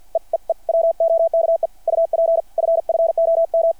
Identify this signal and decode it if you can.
Morse code, heard as a keyed tone, about 53 wpm, 660 Hz. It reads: E E E WOKE UP FFOM
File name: archived / CRSnet